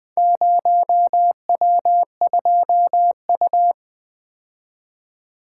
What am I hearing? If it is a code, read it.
Morse code, heard as a keyed tone, about 21 wpm, 700 Hz. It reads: 0W2U
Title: Four Letter-Number Combinations - 20wpm